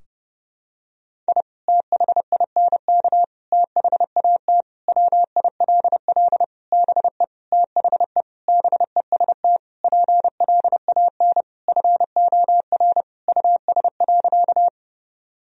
Morse code, heard as a keyed tone, about 30 wpm, 700 Hz. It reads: I THINK THAT WILL BE THE BEST PLAN FOR US.
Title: Sentences from Top 1000 Words - Review Part 2 - 30wpm